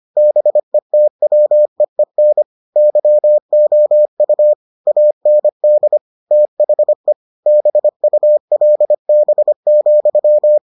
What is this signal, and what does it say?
Morse code, heard as a keyed tone, about 25 wpm, 600 Hz. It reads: BETWEEN YOU AND THE BULB,